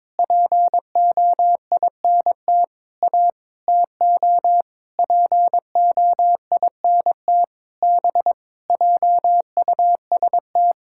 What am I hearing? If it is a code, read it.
Morse code, heard as a keyed tone, about 21 wpm, 700 Hz. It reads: POINT A TO POINT B JUST
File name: practice / 22wpm / Sentences with 8-Word Phrases - 22wpm